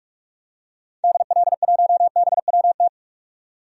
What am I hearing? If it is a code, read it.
Morse code, heard as a keyed tone, about 44 wpm, 700 Hz. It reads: DL1BWT
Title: DXCC-100 Call Signs - 6 Characters - 45wpm